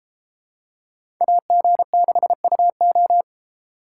Morse code, heard as a keyed tone, about 32 wpm, 700 Hz. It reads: AG6UO